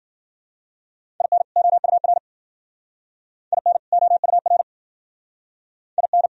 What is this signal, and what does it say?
Morse code, heard as a keyed tone, about 49 wpm, 700 Hz. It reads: IN OUR IN OUR IN
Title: Speed-Racing - Sets of 2 Words 40wpm